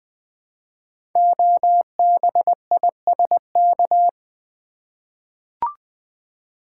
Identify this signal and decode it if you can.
Morse code, heard as a keyed tone, about 20 wpm, 700 Hz. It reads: OBISK
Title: Code Groups - 5 Letters - 20wpm